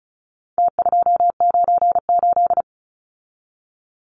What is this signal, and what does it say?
Morse code, heard as a keyed tone, about 34 wpm, 700 Hz. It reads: T298
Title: Numbers - Contest Style - 4-Digit 35wpm